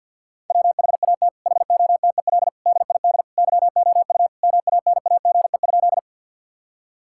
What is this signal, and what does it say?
Morse code, heard as a keyed tone, about 50 wpm, 700 Hz. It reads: WHAT HOTEL DID YOU MANAGE?
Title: Sentences from Top 1600 Words 50wpm